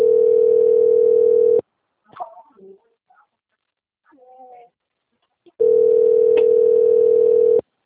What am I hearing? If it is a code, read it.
Ringback tone (North America), 440 + 480 Hz.